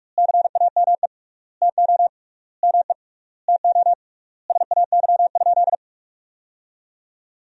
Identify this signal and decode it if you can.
Morse code, heard as a keyed tone, about 41 wpm, 700 Hz. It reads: CAME TO ME TO SAY?
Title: 3x Word-Spacing - Sentences from Top 200 Words - 45wpm